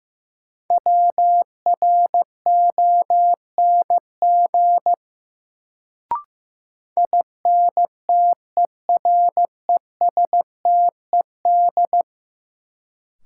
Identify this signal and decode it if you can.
Morse code, heard as a keyed tone, about 15 wpm, 700 Hz. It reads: WRONG INTERESTED
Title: Sentences from Top 900 Words - New Word Drill - 15wpm